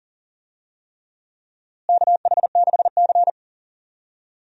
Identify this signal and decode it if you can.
Morse code, heard as a keyed tone, about 40 wpm, 700 Hz. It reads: KH6C
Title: DXCC-100 Call Signs - 4 Characters - Repeated 3x - 40wpm